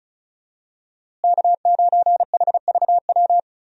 Morse code, heard as a keyed tone, about 34 wpm, 700 Hz. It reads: K9HVW
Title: USA Call Signs - 5 Characters - Repeated 2x - 35wpm